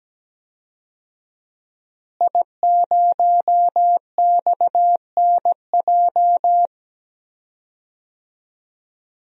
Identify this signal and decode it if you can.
Morse code, heard as a keyed tone, about 17 wpm, 700 Hz. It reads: I0XNJ
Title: DXCC-100 Call Signs - 5 Characters - Repeated 2x - 17wpm